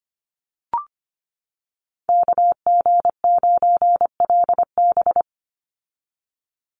Morse code, heard as a keyed tone, about 25 wpm, 700 Hz. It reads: KG9LB
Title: Call Signs - 25wpm